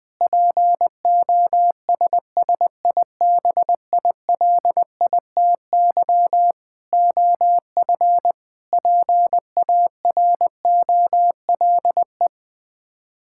Morse code, heard as a keyed tone, about 20 wpm, 700 Hz. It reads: POSSIBILITY OF PAROLE
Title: Sets of 8 Words - 20wpm